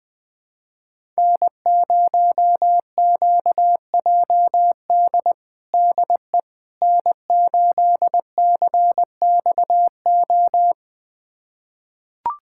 Morse code, heard as a keyed tone, about 20 wpm, 700 Hz. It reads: N0QJD DE N8CXO
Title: USA Callsign DE Callsign - 5 Characters - 20wpm